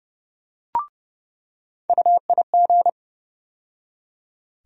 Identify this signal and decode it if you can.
Morse code, heard as a keyed tone, about 30 wpm, 700 Hz. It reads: UIG
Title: USA Call Sign Postfixes - 3 Letters 30wpm